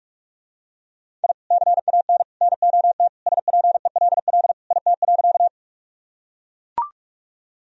Morse code, heard as a keyed tone, about 42 wpm, 700 Hz. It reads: I CAN NOT SPELL IT.
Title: Sentences from Top 1000 Words - Review Part 1 - 45wpm